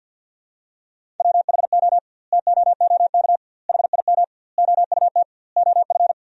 Morse code, heard as a keyed tone, about 50 wpm, 700 Hz. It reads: WHO TOOK HIM OUT OF
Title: 2x Word-Spacing - Sentences from Top 300 Words - 50wpm